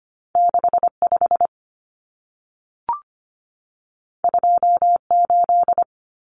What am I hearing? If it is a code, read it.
Morse code, heard as a keyed tone, about 25 wpm, 700 Hz. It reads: 65 28
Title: Numbers - 2-Digit 25wpm